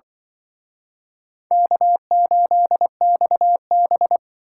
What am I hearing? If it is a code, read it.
Morse code, heard as a keyed tone, about 24 wpm, 700 Hz. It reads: K8XB
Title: Speed Racing - USA 1x2 Call Signs 20wpm